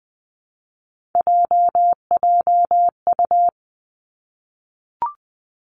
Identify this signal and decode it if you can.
Morse code, heard as a keyed tone, about 20 wpm, 700 Hz. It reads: JJU
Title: Three Challenging Letter-Number Combinations 20wpm